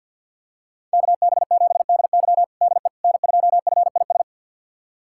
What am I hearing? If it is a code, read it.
Morse code, heard as a keyed tone, about 51 wpm, 700 Hz. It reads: KB7DY DE N2FIS
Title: USA Callsign DE Callsign - 5 Characters - 50wpm